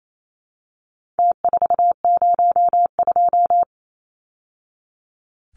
Morse code, heard as a keyed tone, about 28 wpm, 700 Hz. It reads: T402
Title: Numbers - Contest Style - 4-Digit 28wpm